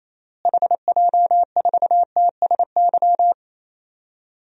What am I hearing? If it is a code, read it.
Morse code, heard as a keyed tone, about 27 wpm, 700 Hz. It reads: HJ4TSY
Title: DXCC-100 Call Signs - 6 Characters - Repeated 3x - 28wpm